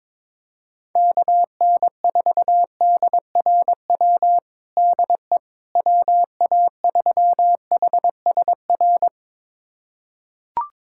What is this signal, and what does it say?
Morse code, heard as a keyed tone, about 21 wpm, 700 Hz. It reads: KN4DRW DE WA3HSR
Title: USA Callsign DE Callsign 22wpm